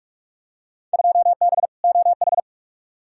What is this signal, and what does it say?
Morse code, heard as a keyed tone, about 42 wpm, 700 Hz. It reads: 2B OH
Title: Field Day Exchanges Encore - 45wpm